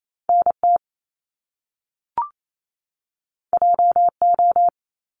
Morse code, heard as a keyed tone, about 28 wpm, 700 Hz. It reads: NT JO
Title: Top Two Letter Combinations in Words 28wpm